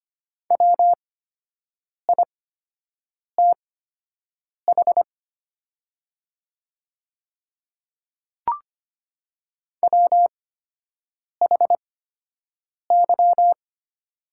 Morse code, heard as a keyed tone, about 25 wpm, spaced as for 7 wpm, 700 Hz. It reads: WITH WHY
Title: 8x Character Spacing - Top 200 Words - 25wpm